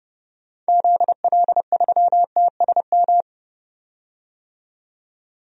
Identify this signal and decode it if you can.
Morse code, heard as a keyed tone, about 30 wpm, 700 Hz. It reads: ZL3TSM
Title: DXCC-100 Call Signs - 6 Characters - 30wpm